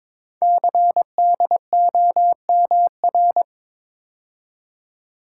Morse code, heard as a keyed tone, about 23 wpm, 700 Hz. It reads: CDOMR
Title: Code Groups - 5 Letters - 22wpm